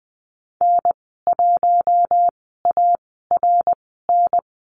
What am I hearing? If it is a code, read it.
Morse code, heard as a keyed tone, about 20 wpm, spaced as for 15 wpm, 700 Hz. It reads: N1ARN
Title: 082 Lesson 14g - Number 2 - Call Signs 20wpm